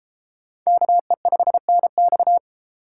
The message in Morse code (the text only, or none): KE5NX